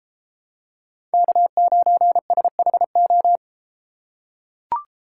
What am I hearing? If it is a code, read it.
Morse code, heard as a keyed tone, about 32 wpm, 700 Hz. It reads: K9SHO